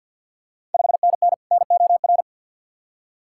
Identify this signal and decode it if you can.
Morse code, heard as a keyed tone, about 48 wpm, 690 Hz. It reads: 5NN NOR
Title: Ontario QSO Party Exchange 50wpm